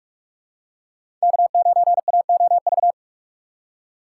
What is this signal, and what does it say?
Morse code, heard as a keyed tone, about 45 wpm, 700 Hz. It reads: K9AOV
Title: USA Call Signs - 5 Characters - 4x Character Spacing - 45wpm